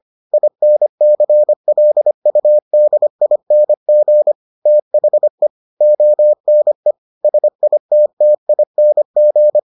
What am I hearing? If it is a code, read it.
Morse code, heard as a keyed tone, about 25 wpm, 600 Hz. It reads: INCLUDING THE ONE SITTING